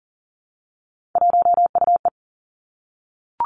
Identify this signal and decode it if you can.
Morse code, heard as a keyed tone, about 38 wpm, 700 Hz. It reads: 1UE
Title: Three Challenging Letter-Number Combinations 40wpm